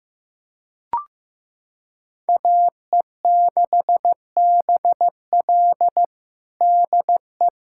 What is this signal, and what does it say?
Morse code, heard as a keyed tone, about 15 wpm, 700 Hz. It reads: AE6BL DE
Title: USA Callsign DE Callsign 15wpm